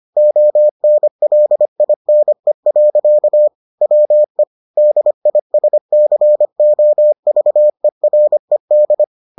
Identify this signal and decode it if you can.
Morse code, heard as a keyed tone, about 25 wpm, 600 Hz. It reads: ONLINE. WE DISCOVERED